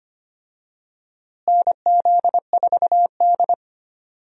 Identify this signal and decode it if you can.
Morse code, heard as a keyed tone, about 25 wpm, 700 Hz. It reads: NZ4D
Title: USA Call Signs - 4 Characters - Repeated 2x - 25wpm